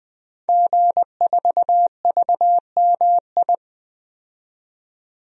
Morse code, heard as a keyed tone, about 21 wpm, 700 Hz. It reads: G4VMI
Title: DXCC-100 Call Signs - 5 Characters - 20wpm